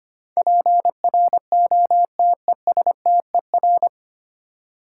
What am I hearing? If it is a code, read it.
Morse code, heard as a keyed tone, about 25 wpm, 700 Hz. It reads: PROTESTER
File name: Nine Letter Words - 25wpm